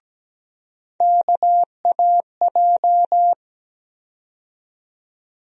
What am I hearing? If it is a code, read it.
Morse code, heard as a keyed tone, about 17 wpm, 700 Hz. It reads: KAJ
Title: USA Call Sign Postfixes - 3 Letters 17wpm